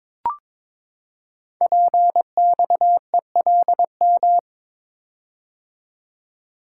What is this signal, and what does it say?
Morse code, heard as a keyed tone, about 22 wpm, 700 Hz. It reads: PXELM
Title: Code Groups - 5 Letters - 22wpm